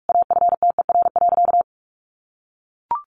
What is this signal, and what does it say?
Morse code, heard as a keyed tone, about 44 wpm, 700 Hz. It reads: AFTER.
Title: Sentences from Top 500 Words 45wpm